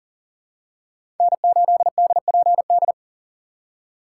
Morse code, heard as a keyed tone, about 40 wpm, 700 Hz. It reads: N8DPD
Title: K1USN SST QSO Practice - 40wpm